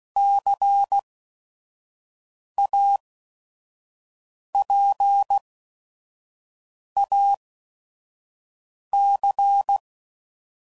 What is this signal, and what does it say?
Morse code, heard as a keyed tone, about 16 wpm, spaced as for 5 wpm, 800 Hz. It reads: CAPAC
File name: CW Code Groups @5WPM / Technical Subjects / XC